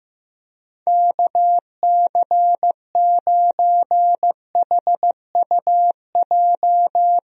Morse code, heard as a keyed tone, about 15 wpm, 700 Hz. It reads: KC9HUJ
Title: USA Call Signs - 6 Characters - 15wpm